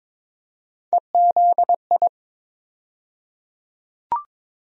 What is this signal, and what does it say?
Morse code, heard as a keyed tone, about 22 wpm, 700 Hz. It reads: EZI
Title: USA Call Sign Postfixes - 3 Letters 22wpm